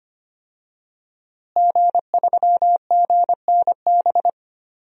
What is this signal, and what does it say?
Morse code, heard as a keyed tone, about 25 wpm, 700 Hz. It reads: G3GNB
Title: DXCC-100 Call Signs - 5 Characters - Repeated 3x - 25wpm